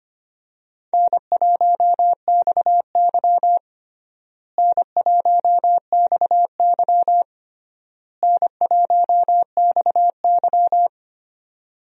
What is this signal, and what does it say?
Morse code, heard as a keyed tone, about 25 wpm, 700 Hz. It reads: N1XY N1XY N1XY
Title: USA Call Signs - 4 Characters - Repeated 3x - 25wpm